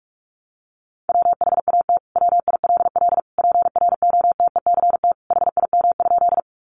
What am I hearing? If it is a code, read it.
Morse code, heard as a keyed tone, about 44 wpm, 700 Hz. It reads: WHAT WILL PROTECT HIM?
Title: Sentences from Top 900 Words 45wpm